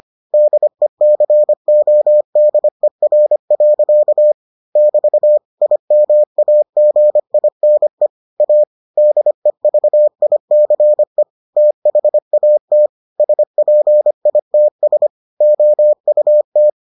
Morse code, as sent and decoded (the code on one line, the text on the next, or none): -.. -.-. --- -.. .-. .-.-.- / -...- / .. -- .- --. .. -. / .- / -.. ...- .. -.-. / - .... .- - / ... .--. .. - ... / --- ..- -
DECODER. <BT> IMAGINE A DEVICE THAT SPITS OUT